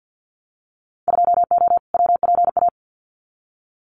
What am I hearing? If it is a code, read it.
Morse code, heard as a keyed tone, about 50 wpm, 700 Hz. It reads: 2O WPA
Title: Winter Field Day Exchanges 50wpm